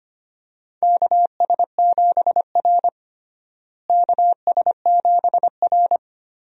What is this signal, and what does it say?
Morse code, heard as a keyed tone, about 25 wpm, 700 Hz. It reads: KS7R KS7R
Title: USA Call Signs - 4 Characters - Repeated 3x - 25wpm